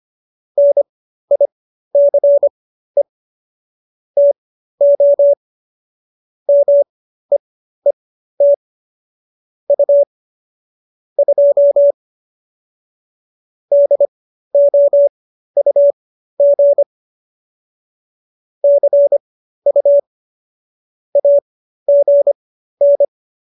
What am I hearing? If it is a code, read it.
Morse code, heard as a keyed tone, about 25 wpm, spaced as for 13 wpm, 570 Hz. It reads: NICE TO MEET U 2 DOUG CU AGN